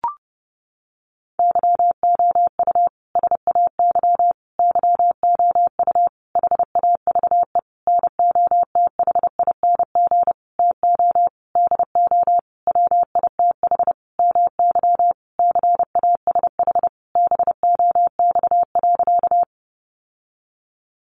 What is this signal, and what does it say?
Morse code, heard as a keyed tone, about 30 wpm, 700 Hz. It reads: YOU SAY YOU HAVE NOTHING TO DO WITH MY CASH BOX.
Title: Sentences from Top 700 Words 30wpm